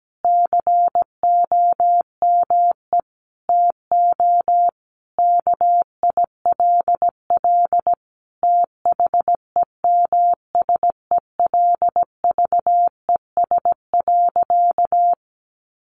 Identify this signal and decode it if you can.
Morse code, heard as a keyed tone, about 17 wpm, 700 Hz. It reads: COME TO KILL THEMSELVES.
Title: Sentences from Top 1400 Words 17wpm